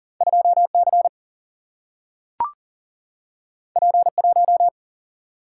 Morse code, heard as a keyed tone, about 39 wpm, 700 Hz. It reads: 2 C P 1
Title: Single-Letter Single-Number 40wpm